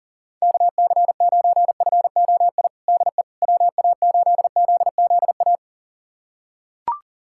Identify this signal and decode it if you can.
Morse code, heard as a keyed tone, about 40 wpm, 690 Hz. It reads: KC9FOI DE WA8ZZA
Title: USA Callsign DE Callsign - 6 Characters - 40wpm